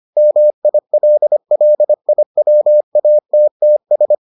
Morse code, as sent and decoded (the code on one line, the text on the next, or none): -- .. .-.. .-.. .. .-- .- - - ...
MILLIWATTS